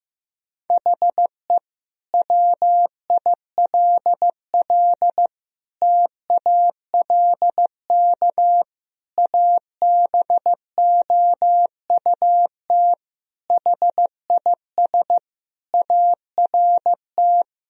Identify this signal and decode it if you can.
Morse code, heard as a keyed tone, about 15 wpm, 700 Hz. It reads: HE WILL TALK ABOUT HIS ART
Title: Sentences from Top 1000 Words - Review Part 3 - 15wpm